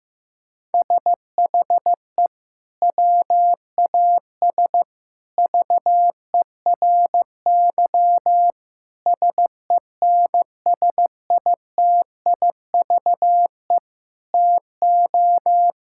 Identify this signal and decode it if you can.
Morse code, heard as a keyed tone, about 16 wpm, 700 Hz. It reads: SHE WAS VERY SENSITIVE TO
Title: Sentences from Top 1600 Words 15wpm